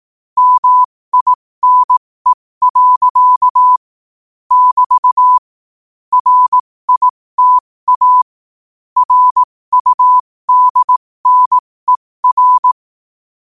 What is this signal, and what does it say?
Morse code, heard as a keyed tone, about 18 wpm, spaced as for 15 wpm, 1000 Hz. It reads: MINE. <BT> RITA RUDNER
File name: qotdfiles / 15wpm